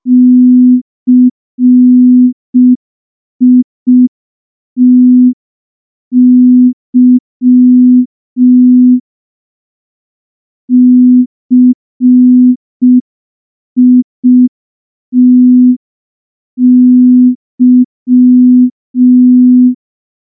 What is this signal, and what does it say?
Morse code, heard as a keyed tone, about 5 wpm, 250 Hz. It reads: CITY CITY
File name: morse / morse-training-05WPM